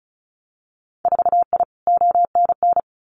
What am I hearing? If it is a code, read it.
Morse code, heard as a keyed tone, about 34 wpm, 700 Hz. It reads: 4I ONN
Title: Winter Field Day Exchanges Encore 35wpm